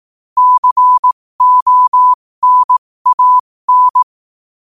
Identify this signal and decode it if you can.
Morse code, heard as a keyed tone, about 18 wpm, spaced as for 15 wpm, 1000 Hz. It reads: CONAN
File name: qotdfiles / 15wpm